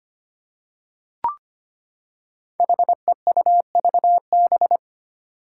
Morse code, heard as a keyed tone, about 25 wpm, 700 Hz. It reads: HEUVB
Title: Five Challenging Letter-Number Combinations 25wpm